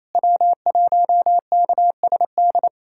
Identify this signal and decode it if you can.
Morse code, heard as a keyed tone, about 28 wpm, 700 Hz. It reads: W1KSD
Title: USA Callsign DE Callsign 28wpm